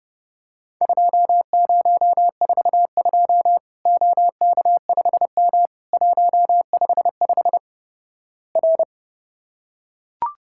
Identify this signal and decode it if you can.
Morse code, heard as a keyed tone, about 30 wpm, 700 Hz. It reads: 2042 OK5M 155 R
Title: Worked All Europe DX Contest - QTC Exchange - 30wpm